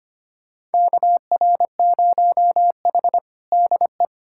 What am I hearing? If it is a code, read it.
Morse code, heard as a keyed tone, about 25 wpm, 700 Hz. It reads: KR0H DE
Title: USA Callsign DE Callsign - 4 Characters - 25wpm